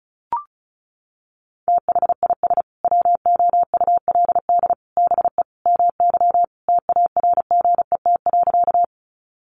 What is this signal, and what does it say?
Morse code, heard as a keyed tone, about 34 wpm, 700 Hz. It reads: THIS WOULD BE MY TARGET.
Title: Sentences from Top 1200 Words 35wpm